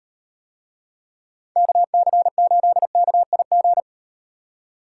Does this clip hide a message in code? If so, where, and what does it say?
Yes: Morse code from 1.6 to 3.8 s, heard as a keyed tone, about 38 wpm, 700 Hz. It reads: KC8KIG